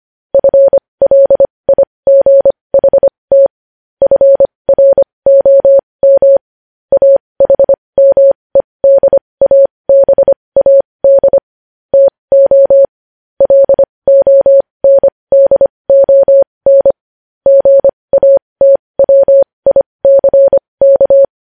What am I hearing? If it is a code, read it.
Morse code, heard as a keyed tone, about 25 wpm, spaced as for 20 wpm, 550 Hz. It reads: FLIGHT FROM AHMEDABAD TO LONDON GATWICK
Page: News headlines in Morse code at 20 wpm